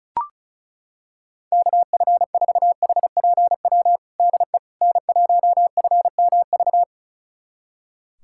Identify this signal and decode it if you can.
Morse code, heard as a keyed tone, about 36 wpm, 700 Hz. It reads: KF4HPW DE N1FMV
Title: USA Callsign DE Callsign 35wpm